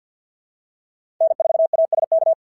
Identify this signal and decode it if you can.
Morse code, heard as a keyed tone, about 51 wpm, 690 Hz. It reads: N4ASK
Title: SOTA QSO Practice 50wpm